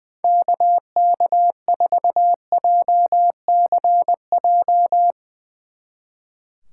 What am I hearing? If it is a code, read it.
Morse code, heard as a keyed tone, about 22 wpm, 700 Hz. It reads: KK4JCJ
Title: Speed Racing - USA 2x3 Call Signs 20wpm